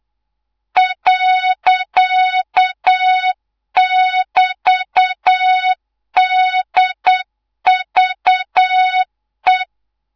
Morse code, heard as a keyed tone, about 8 wpm, 3700 Hz, with noise. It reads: .<BT>DVE
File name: Morze-HA2MN / hangp / 2 03 GYAKORLAT